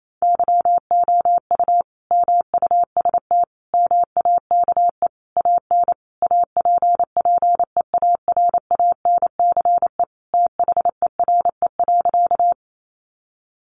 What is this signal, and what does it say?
Morse code, heard as a keyed tone, about 28 wpm, 700 Hz. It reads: YOU MUST MAKE AN APPEARANCE THERE.
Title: Sentences from Top 1100 Words 28wpm